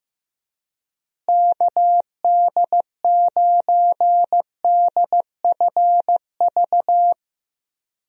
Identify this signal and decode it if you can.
Morse code, heard as a keyed tone, about 15 wpm, 700 Hz. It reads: KD9DFV
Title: USA Call Signs - 6 Characters - Repeated 3x - 15wpm